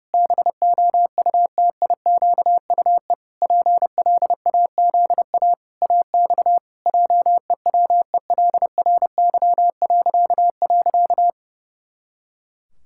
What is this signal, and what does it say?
Morse code, heard as a keyed tone, about 30 wpm, 700 Hz. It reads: BOUTIQUE PLAZA AX JEWELRY..
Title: Pangram Challenge v2 - 30wpm